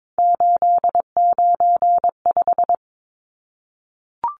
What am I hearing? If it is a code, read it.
Morse code, heard as a keyed tone, about 22 wpm, 700 Hz. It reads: 895
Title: Numbers - 3-Digit 22wpm